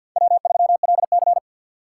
Morse code, heard as a keyed tone, about 51 wpm, 700 Hz. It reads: W3LC
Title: USA Call Signs - 4 Characters - 2x Character Spacing - 50wpm